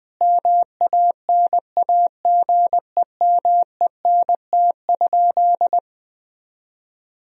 Morse code, heard as a keyed tone, about 20 wpm, 700 Hz. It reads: MANAGEMENT?
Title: Sentences from Top 1000 Words - Review Part 4 - 20wpm